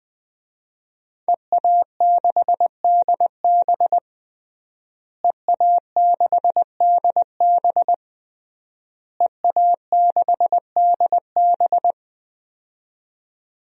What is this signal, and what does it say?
Morse code, heard as a keyed tone, about 20 wpm, 700 Hz. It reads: EA6DB EA6DB EA6DB
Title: DXCC-100 Call Signs - 5 Characters - Repeated 3x - 20wpm